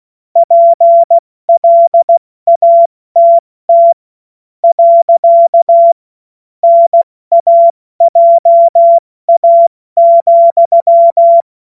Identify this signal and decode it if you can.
Morse code, heard as a keyed tone, about 16 wpm, 680 Hz. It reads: PLATT . NAJA,
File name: Morsekurs / Witze / Witz 02 April 2024 70 F80